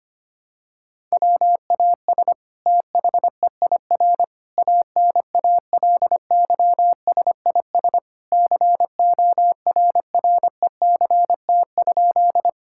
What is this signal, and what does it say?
Morse code, heard as a keyed tone, about 25 wpm, 700 Hz. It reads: WAS THEIR ANALYSIS CORRECT?
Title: Sentences from Top 1000 Words - Review Part 5 - 25wpm